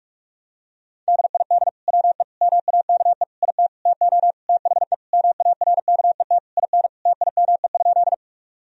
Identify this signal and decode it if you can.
Morse code, heard as a keyed tone, about 46 wpm, 700 Hz. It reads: DID WE MAKE IT TO THE MARKET IN TIME?
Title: Sentences from Top 1000 Words - Review Part 3 - 45wpm